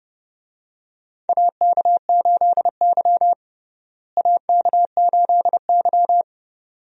Morse code, heard as a keyed tone, about 32 wpm, 700 Hz. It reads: AK8Y AK8Y
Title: USA Call Signs - 4 Characters - Repeated 2x - 30wpm